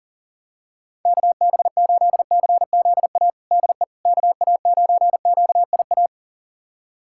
Morse code, heard as a keyed tone, about 40 wpm, 700 Hz. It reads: KB8CZA DE KA9QIA
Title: USA Callsign DE Callsign 40wpm